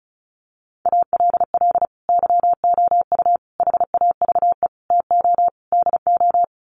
Morse code, heard as a keyed tone, about 34 wpm, 700 Hz. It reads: ALL YOU HAVE TO DO
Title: Sets of 5 Words 35wpm